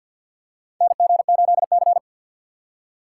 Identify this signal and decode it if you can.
Morse code, heard as a keyed tone, about 51 wpm, 700 Hz. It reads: NG8C
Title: USA Call Signs - 4 Characters - 2x Character Spacing - 50wpm